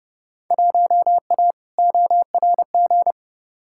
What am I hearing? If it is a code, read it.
Morse code, heard as a keyed tone, about 29 wpm, 700 Hz. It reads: 1A ORG
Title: Field Day Exchanges Encore - 30wpm